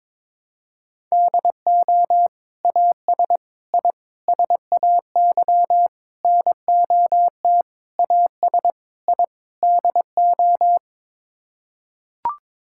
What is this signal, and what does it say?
Morse code, heard as a keyed tone, about 22 wpm, 700 Hz. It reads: DO AS I SAY NOT AS I DO
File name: practice / 22wpm / Sets of 8 Words - 22wpm